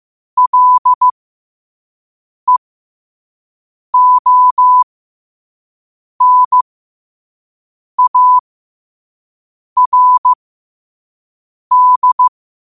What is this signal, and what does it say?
Morse code, heard as a keyed tone, about 15 wpm, spaced as for 5 wpm, 1000 Hz. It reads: LEONARD
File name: qotdfiles / 5wpm